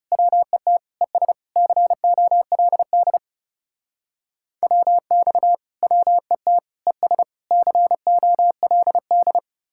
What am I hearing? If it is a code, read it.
Morse code, heard as a keyed tone, about 36 wpm, 700 Hz. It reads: WET ES COLD WX WET ES COLD
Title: Speed-Racing Common QSO Phrases - 30wpm